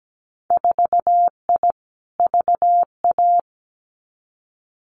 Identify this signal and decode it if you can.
Morse code, heard as a keyed tone, about 17 wpm, 700 Hz. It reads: 4I VA
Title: Winter Field Day Exchanges 17wpm